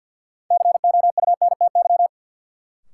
Morse code, heard as a keyed tone, about 52 wpm, 700 Hz. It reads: COUNTY
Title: Sentences from Top 1200 Words - New Word Drill - 50wpm